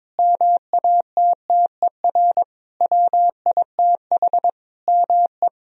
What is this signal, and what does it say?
Morse code, heard as a keyed tone, about 22 wpm, 700 Hz. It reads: MATTER WITH ME
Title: Sentences from Top 1000 Words - Review Part 4 - 22wpm